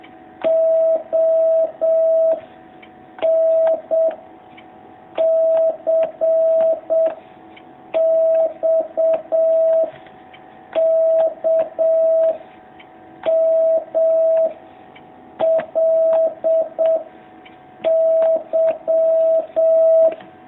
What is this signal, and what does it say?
Morse code, heard as a keyed tone, about 7 wpm, 650 Hz. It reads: O N C X K M L Y
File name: jh8cbh / 2019cw